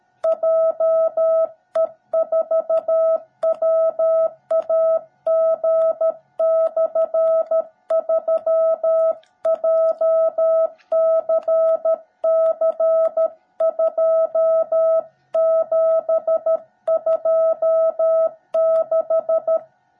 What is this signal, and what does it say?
Morse code, heard as a keyed tone, about 13 wpm, 660 Hz. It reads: JE4WAG/3 JCC2726
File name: jh8cbh / 2021cw / cw17b